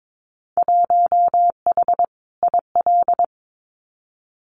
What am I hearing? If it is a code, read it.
Morse code, heard as a keyed tone, about 22 wpm, 700 Hz. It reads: 1H IL
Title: Winter Field Day Exchanges 22wpm